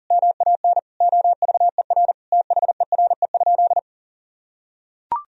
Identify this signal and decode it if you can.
Morse code, heard as a keyed tone, about 40 wpm, 700 Hz. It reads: MAN OVER THERE?
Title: Sentences from Top 200 Words and Abbreviations 40wpm